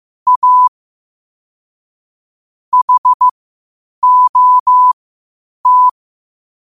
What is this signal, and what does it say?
Morse code, heard as a keyed tone, about 15 wpm, spaced as for 8 wpm, 1000 Hz. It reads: A HOT
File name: qotdfiles / 8wpm